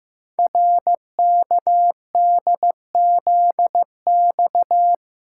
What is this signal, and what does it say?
Morse code, heard as a keyed tone, about 15 wpm, 700 Hz. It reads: RKDZX
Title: Code Groups - 5 Letters - 15wpm